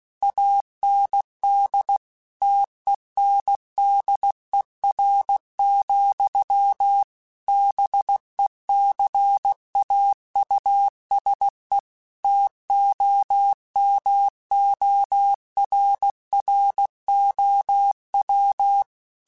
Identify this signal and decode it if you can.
Morse code, heard as a keyed tone, about 16 wpm, 800 Hz. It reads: AND TENDER, BECAUSE TOMORROW
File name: Chuckles / Chuckles-mp3s / SoftWords @15